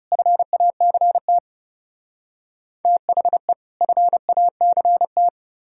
Morse code, heard as a keyed tone, about 37 wpm, 700 Hz. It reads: FACT THE FACT
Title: Speed-Racing - Sets of 2 Words 30wpm